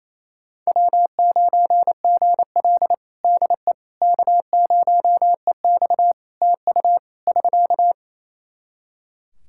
Morse code, heard as a keyed tone, about 28 wpm, 700 Hz. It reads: W9GL DE K0EX TU <SK>
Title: Practice QSO - Exchange 4 - Ending - 28wpm